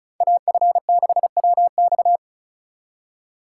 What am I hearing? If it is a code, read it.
Morse code, heard as a keyed tone, about 43 wpm, 700 Hz. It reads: AF6WX
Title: Speed Racing - USA 2x2 Call Signs 35wpm